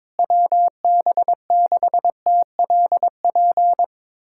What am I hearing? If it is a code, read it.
Morse code, heard as a keyed tone, about 22 wpm, 700 Hz. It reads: WB6TLP
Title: USA Callsign DE Callsign - 6 Characters - 22wpm